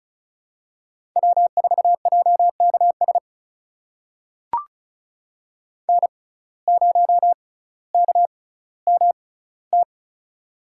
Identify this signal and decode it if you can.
Morse code, heard as a keyed tone, about 36 wpm, spaced as for 18 wpm, 700 Hz. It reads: W4JKS N0KMT
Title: USA Call Signs - 5 Characters - 6x Character Spacing - 35wpm